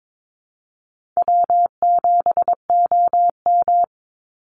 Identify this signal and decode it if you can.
Morse code, heard as a keyed tone, about 22 wpm, 700 Hz. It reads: W7OM
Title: Easy Call Signs Repeated 2x - 22wpm